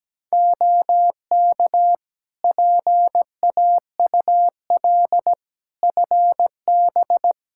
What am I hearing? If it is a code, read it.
Morse code, heard as a keyed tone, about 17 wpm, 700 Hz. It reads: OK PAUL FB
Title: K1USN SST QSO Practice 17wpm